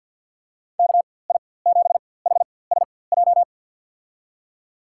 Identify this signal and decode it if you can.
Morse code, heard as a keyed tone, about 46 wpm, spaced as for 23 wpm, 700 Hz. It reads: KI7HSJ
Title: USA Call Signs - 4x Character Spacing - 50wpm